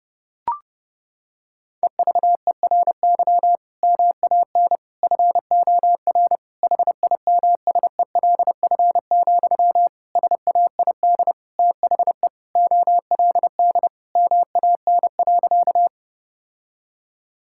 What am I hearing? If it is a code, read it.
Morse code, heard as a keyed tone, about 30 wpm, 700 Hz. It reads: EVERY MAN FOR HIMSELF, SAID THE OLD MAN.
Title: Sentences from Top 1600 Words 30wpm